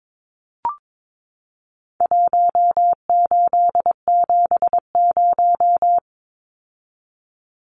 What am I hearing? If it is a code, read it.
Morse code, heard as a keyed tone, about 22 wpm, 700 Hz. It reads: 1870
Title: Numbers - Contest Style - 4-Digit 22wpm